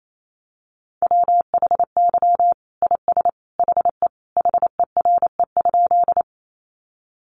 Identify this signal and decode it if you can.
Morse code, heard as a keyed tone, about 28 wpm, 700 Hz. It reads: WHY IS HE HERE?
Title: Sentences from Top 200 Words 28wpm